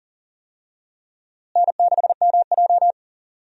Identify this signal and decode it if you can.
Morse code, heard as a keyed tone, about 40 wpm, 700 Hz. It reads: N6MJ
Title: USA Call Signs - 4 Characters - Repeated 3x - 40wpm